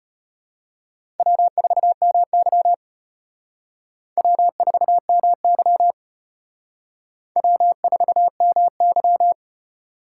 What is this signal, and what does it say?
Morse code, heard as a keyed tone, about 32 wpm, 700 Hz. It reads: W4MY W4MY W4MY